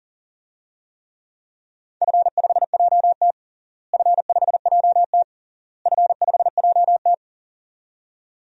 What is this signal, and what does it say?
Morse code, heard as a keyed tone, about 40 wpm, 700 Hz. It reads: F5JT F5JT F5JT
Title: DXCC-100 Call Signs - 4 Characters - Repeated 3x - 40wpm